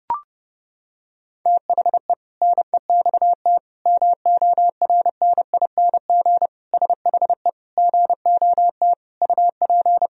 Morse code, heard as a keyed tone, about 32 wpm, 700 Hz. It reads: THE NEXT MORNING SHE GOT UP
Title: Sentences from Top 1000 Words - Review Part 2 - 30wpm